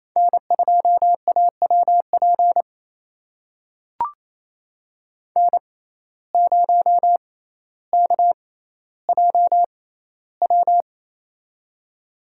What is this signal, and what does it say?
Morse code, heard as a keyed tone, about 28 wpm, spaced as for 14 wpm, 700 Hz. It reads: N2AWP N0KJW
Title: USA Call Signs - 5 Characters - 6x Character Spacing - 28wpm